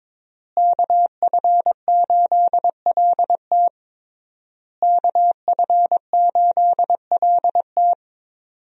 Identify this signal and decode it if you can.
Morse code, heard as a keyed tone, about 22 wpm, 700 Hz. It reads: KF8LT KF8LT
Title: USA Call Signs - 5 Characters - Repeated 3x - 22wpm